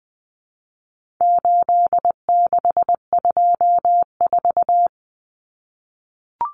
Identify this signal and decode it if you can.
Morse code, heard as a keyed tone, about 20 wpm, 700 Hz. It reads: 8624
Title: Common Cut Numbers - 4-Digit 20wpm